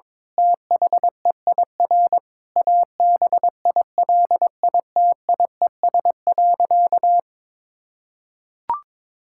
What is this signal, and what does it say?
Morse code, heard as a keyed tone, about 22 wpm, 700 Hz. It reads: THEIR ABILITIES.
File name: Difficult Sentences with Hints - 22wpm